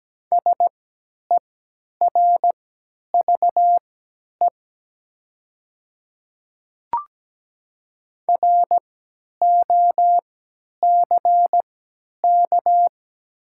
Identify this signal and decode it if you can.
Morse code, heard as a keyed tone, about 17 wpm, spaced as for 10 wpm, 700 Hz. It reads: SERVE ROCK
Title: 3x Character Spacing - Top 500 Words - 17wpm